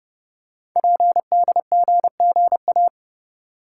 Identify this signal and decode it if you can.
Morse code, heard as a keyed tone, about 30 wpm, 700 Hz. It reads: PDGGA
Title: Code Groups - 5 Letters - 30wpm